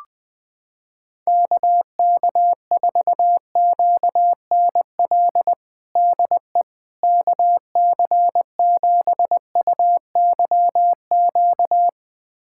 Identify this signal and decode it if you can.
Morse code, heard as a keyed tone, about 21 wpm, 700 Hz. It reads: KK4QNL DE KC7UYQ
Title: USA Callsign DE Callsign - 6 Characters - 20wpm